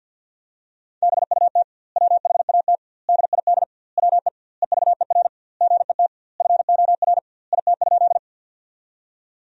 Morse code, heard as a keyed tone, about 50 wpm, 700 Hz. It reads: BUT WHAT DID WE EVER GET FOR IT?
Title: 2x Word-Spacing - Sentences from Top 400 Words - 50wpm